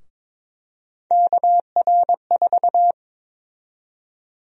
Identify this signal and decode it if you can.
Morse code, heard as a keyed tone, about 22 wpm, 700 Hz. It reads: KR4
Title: USA Call Sign Prefixes - 22wpm